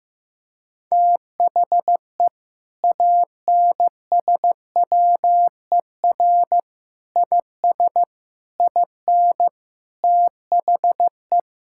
Morse code, heard as a keyed tone, about 15 wpm, 700 Hz. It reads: THE ANSWER IS IN THE
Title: Sentences from Top 1000 Words - Review Part 5 - 15wpm